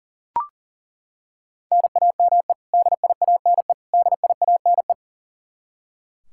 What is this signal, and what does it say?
Morse code, heard as a keyed tone, about 40 wpm, 700 Hz. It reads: NAME DIANE DIANE
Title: Practice QSO - Exchange 1 - Beginning - 40wpm